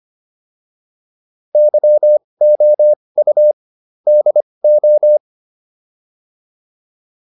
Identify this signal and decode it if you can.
Morse code, heard as a keyed tone, about 25 wpm, spaced as for 20 wpm, 600 Hz. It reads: YOU DO